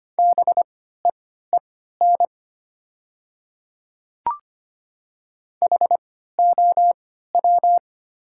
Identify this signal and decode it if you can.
Morse code, heard as a keyed tone, about 25 wpm, spaced as for 14 wpm, 700 Hz. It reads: BEEN HOW
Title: 3x Character Spacing - Top 100 Words - 25wpm